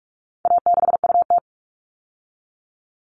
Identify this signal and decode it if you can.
Morse code, heard as a keyed tone, about 44 wpm, 700 Hz. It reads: A6UT
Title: All Cut Numbers - 4-Digit 45wpm